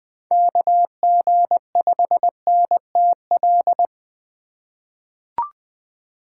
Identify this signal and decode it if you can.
Morse code, heard as a keyed tone, about 20 wpm, 700 Hz. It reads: KG5NTL
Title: USA Call Signs - 6 Characters - 4x Character Spacing - 20wpm